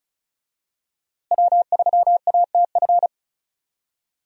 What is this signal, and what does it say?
Morse code, heard as a keyed tone, about 34 wpm, 700 Hz. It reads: W3ATF
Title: USA Call Signs - 5 Characters - Repeated 2x - 35wpm